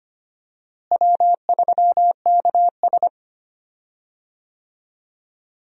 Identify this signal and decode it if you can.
Morse code, heard as a keyed tone, about 25 wpm, 700 Hz. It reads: W3KS
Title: USA Call Signs - 4 Characters - Repeated 3x - 25wpm